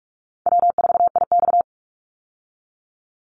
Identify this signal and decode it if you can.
Morse code, heard as a keyed tone, about 44 wpm, 700 Hz. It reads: W4IX
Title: Call Signs Repeated 2x - 45wpm